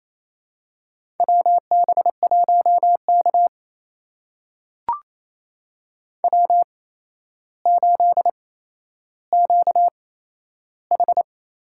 Morse code, heard as a keyed tone, about 28 wpm, spaced as for 12 wpm, 700 Hz. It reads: WB1K W8QH
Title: USA Call Signs - 4 Characters - 8x Character Spacing - 28wpm